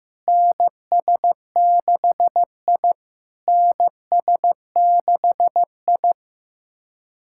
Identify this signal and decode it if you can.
Morse code, heard as a keyed tone, about 15 wpm, 700 Hz. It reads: NS6I NS6I
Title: SOTA QSO Practice 15wpm